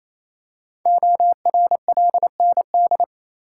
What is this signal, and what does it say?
Morse code, heard as a keyed tone, about 28 wpm, 700 Hz. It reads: ORLND
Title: Code Groups - 5 Letters - 28wpm